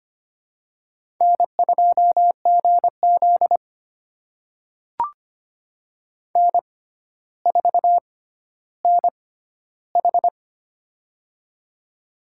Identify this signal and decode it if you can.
Morse code, heard as a keyed tone, about 25 wpm, spaced as for 13 wpm, 700 Hz. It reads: N2GZ N4NH
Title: USA Call Signs - 4 Characters - 6x Character Spacing - 25wpm